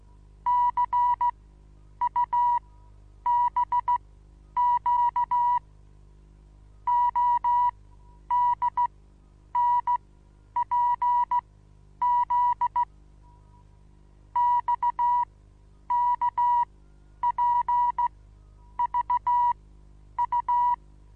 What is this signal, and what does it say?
Morse code, heard as a keyed tone, about 16 wpm, spaced as for 9 wpm, 970 Hz. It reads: CUBQ ODNPZ XKPVU
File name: Onlinelehrgang / morsen / DJ4UF Morsen25